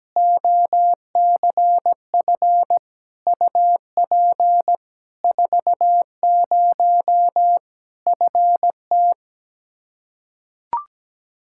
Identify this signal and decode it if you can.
Morse code, heard as a keyed tone, about 17 wpm, 700 Hz. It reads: OCF UP 40 FT
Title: Practice QSO - Exchange 2 - Info - 17wpm